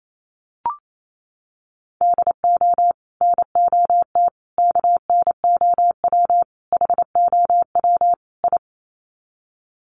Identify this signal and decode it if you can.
Morse code, heard as a keyed tone, about 28 wpm, 700 Hz. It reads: DO NOT KNOW HOW I
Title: Sets of 5 Words Encore2 28wpm